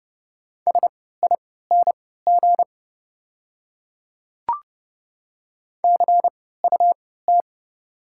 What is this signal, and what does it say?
Morse code, heard as a keyed tone, about 29 wpm, spaced as for 17 wpm, 690 Hz. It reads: SING CUT
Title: 3x Character Spacing - Top 500 Words - 30wpm